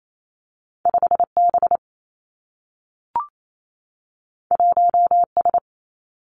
Morse code, heard as a keyed tone, about 28 wpm, 700 Hz. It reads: 5 B 1 S
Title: Two Challenging Letter-Number Combinations 28wpm